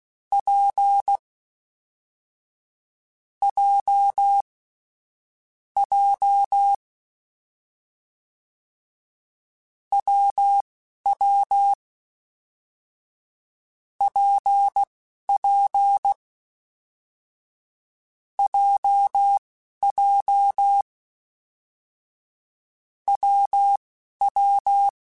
Morse code, heard as a keyed tone, about 16 wpm, spaced as for 11 wpm, 800 Hz. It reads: P J J WW PP JJ WW
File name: Difficult Combos / Difficult Combos-mp3s / WPJ WPJ1 @10